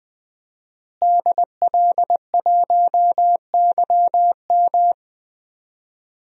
Morse code, heard as a keyed tone, about 20 wpm, 700 Hz. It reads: DL1YM
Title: DXCC-100 Call Signs - 5 Characters - Repeated 2x - 20wpm